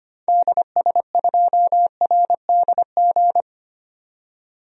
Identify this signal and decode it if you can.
Morse code, heard as a keyed tone, about 25 wpm, 700 Hz. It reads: DS2RDG
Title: DXCC-100 Call Signs - 6 Characters - 25wpm